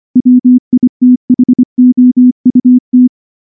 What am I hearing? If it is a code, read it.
Morse code, heard as a keyed tone, about 24 wpm, 250 Hz. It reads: WITHOUT
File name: morse / morse-training-25WPM